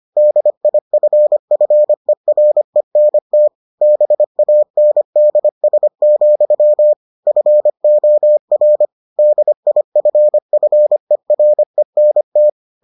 Morse code, heard as a keyed tone, about 25 wpm, 600 Hz. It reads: DIFFERENT BANDS, FOR DIFFERENT